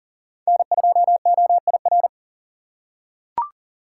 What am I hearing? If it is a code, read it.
Morse code, heard as a keyed tone, about 38 wpm, 700 Hz. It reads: N2OIR
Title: USA Call Signs - 3x Character Spacing - 40wpm